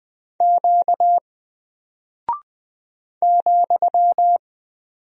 Morse code, heard as a keyed tone, about 20 wpm, 700 Hz. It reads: Q ,
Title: Single Character (Letter:Number:Punctuation:Prosign) Alt 20wpm